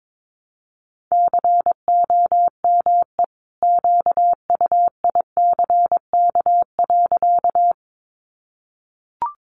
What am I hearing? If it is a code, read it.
Morse code, heard as a keyed tone, about 22 wpm, 700 Hz. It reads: COME QUICK.
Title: Sentences from Top 500 Words 22wpm